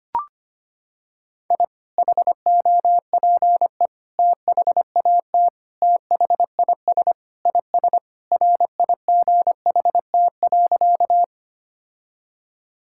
Morse code, heard as a keyed tone, about 25 wpm, 700 Hz. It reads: I HOPE THAT THIS IS RIGHT.
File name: practice / 25wpm / Sentences with 3-Word Phrases - 25wpm